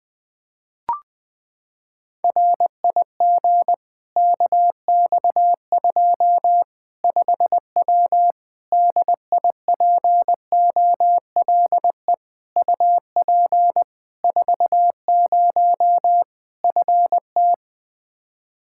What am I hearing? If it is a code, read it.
Morse code, heard as a keyed tone, about 20 wpm, 700 Hz. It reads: RIG KX2 5W DIPOLE UP 40 FT
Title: Practice QSO - Exchange 2 - Info - 20wpm